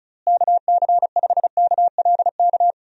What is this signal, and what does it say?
Morse code, heard as a keyed tone, about 36 wpm, 700 Hz. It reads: KC5KLK
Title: USA Call Signs - 6 Characters - Repeated 2x - 35wpm